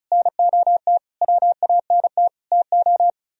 Morse code, heard as a keyed tone, about 41 wpm, 700 Hz. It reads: NOT WANT TO
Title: Speed-Racing - Sets of 5 Words 35wpm